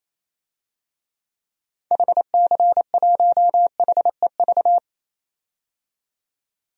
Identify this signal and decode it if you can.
Morse code, heard as a keyed tone, about 28 wpm, 700 Hz. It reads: HC1HEV
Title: DXCC-100 Call Signs - 6 Characters - 28wpm